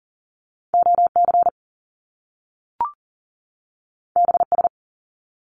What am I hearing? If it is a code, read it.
Morse code, heard as a keyed tone, about 40 wpm, 700 Hz. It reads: OC BS
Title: Top Two Letter Combinations in Words 40wpm